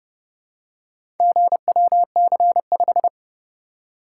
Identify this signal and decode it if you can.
Morse code, heard as a keyed tone, about 29 wpm, 700 Hz. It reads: GWC5